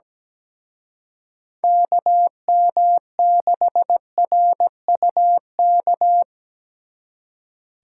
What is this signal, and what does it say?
Morse code, heard as a keyed tone, about 20 wpm, 700 Hz. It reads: KM6RUK
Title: Speed Racing - USA Call Signs 17wpm